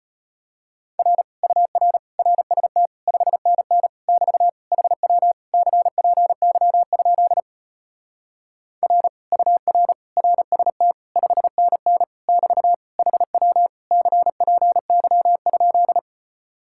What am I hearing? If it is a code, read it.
Morse code, heard as a keyed tone, about 33 wpm, 700 Hz. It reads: R UR RST 5NN <BT> HW CPY? R UR RST 5NN <BT> HW CPY?